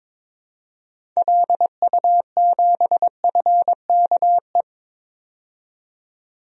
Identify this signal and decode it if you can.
Morse code, heard as a keyed tone, about 22 wpm, 700 Hz. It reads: LU7FKE
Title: DXCC-100 Call Signs - 6 Characters - 22wpm